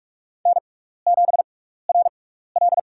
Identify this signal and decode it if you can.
Morse code, heard as a keyed tone, about 43 wpm, spaced as for 15 wpm, 700 Hz. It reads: N7RL
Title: USA Call Signs - 4 Characters - 6x Character Spacing - 45wpm